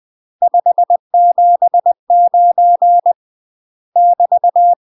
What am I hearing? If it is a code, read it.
Morse code, heard as a keyed tone, about 20 wpm, 700 Hz. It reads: 579 <BT>
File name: UK MASTER QSOs TRAINING / 20WPM / UK MASTER QSOs 0068